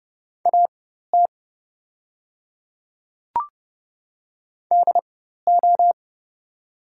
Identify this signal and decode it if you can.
Morse code, heard as a keyed tone, about 30 wpm, spaced as for 14 wpm, 690 Hz. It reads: AT DO
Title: 4x Character Spacing - Top 100 Words - 30wpm